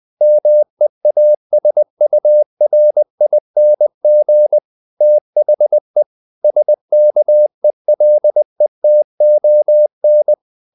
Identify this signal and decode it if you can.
Morse code, heard as a keyed tone, about 20 wpm, 600 Hz. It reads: MEASURING THE SKELETON
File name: the old man and the sea part 2 20wpm